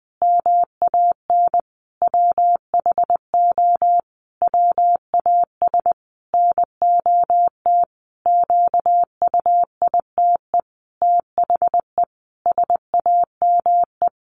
Morse code, heard as a keyed tone, about 20 wpm, 700 Hz. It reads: MAN WHO WAS NOT QUITE THE SAME
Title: A Sound Of Thunder 20wpm